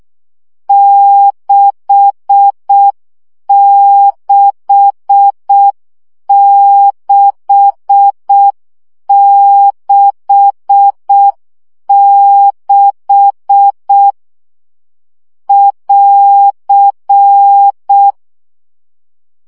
Morse code, heard as a keyed tone, about 6 wpm, 800 Hz. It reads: 66666 <AR>
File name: CW gyakorlas 6 5 baud